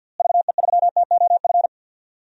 Morse code, heard as a keyed tone, about 46 wpm, 700 Hz. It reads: VE3TOF
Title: DXCC-100 Call Signs - 6 Characters - Repeated 3x - 50wpm